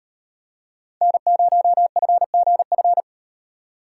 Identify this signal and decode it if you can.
Morse code, heard as a keyed tone, about 38 wpm, 700 Hz. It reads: N0FGF